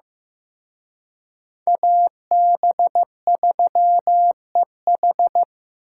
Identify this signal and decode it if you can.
Morse code, heard as a keyed tone, about 18 wpm, 700 Hz. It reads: AB3EH
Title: Speed Racing - USA 2x2 Call Signs 15wpm